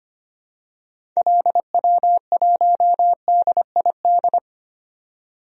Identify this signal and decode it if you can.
Morse code, heard as a keyed tone, about 25 wpm, 700 Hz. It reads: LW1DID
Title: DXCC-100 Call Signs - 6 Characters - 25wpm